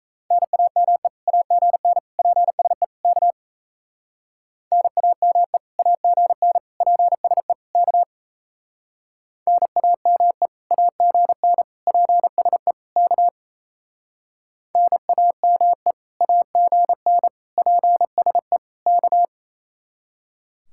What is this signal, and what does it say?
Morse code, heard as a keyed tone, about 34 wpm, 700 Hz. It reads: NAME AGN PSE K NAME AGN PSE K NAME AGN PSE K NAME AGN PSE K
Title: Speed-Racing Common QSO Phrases - 28wpm